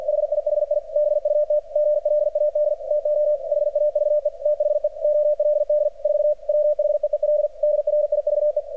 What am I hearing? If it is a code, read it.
Morse code, heard as a keyed tone, about 44 wpm, 600 Hz. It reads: HIHI BUT GLAD TO HAVE THE OLD 4 WHEEL DRIVE
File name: archived / CRSnet